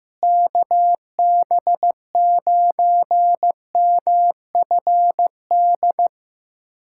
Morse code, heard as a keyed tone, about 15 wpm, 700 Hz. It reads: KB9MFD
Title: USA Callsign DE Callsign - 6 Characters - 15wpm